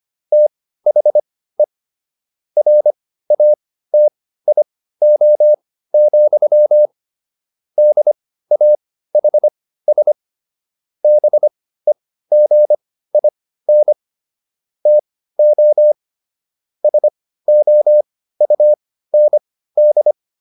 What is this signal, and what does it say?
Morse code, heard as a keyed tone, about 25 wpm, spaced as for 15 wpm, 600 Hz. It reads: THE RATIO, DAHS BEGIN TO SOUND